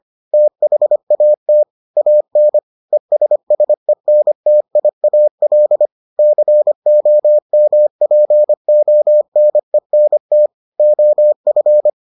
Morse code, heard as a keyed tone, about 25 wpm, 600 Hz. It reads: THAT AN ESSENTIAL COMPONENT OF